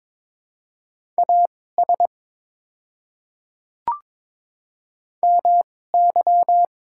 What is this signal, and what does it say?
Morse code, heard as a keyed tone, about 22 wpm, spaced as for 16 wpm, 690 Hz. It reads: AS MY
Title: 2x Character Spacing - Top 100 Words - 22wpm